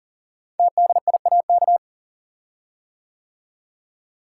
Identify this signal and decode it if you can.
Morse code, heard as a keyed tone, about 40 wpm, 700 Hz. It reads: TDIAK
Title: Code Groups - 5 Letters - 40wpm